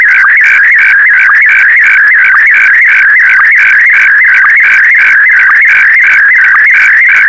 SSTV (slow-scan television), Scottie DX.
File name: sstv / SDX-VK4MSL